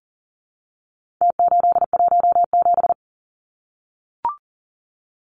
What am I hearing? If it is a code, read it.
Morse code, heard as a keyed tone, about 38 wpm, 700 Hz. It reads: T817
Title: Numbers - Contest Style - 4-Digit 40wpm